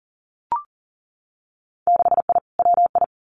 Morse code, heard as a keyed tone, about 40 wpm, 700 Hz. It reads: 6I WI
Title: Winter Field Day Exchanges Encore 40wpm